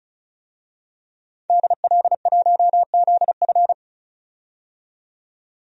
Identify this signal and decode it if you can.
Morse code, heard as a keyed tone, about 36 wpm, 700 Hz. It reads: DL1ZF
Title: DXCC-100 Call Signs - 5 Characters - 35wpm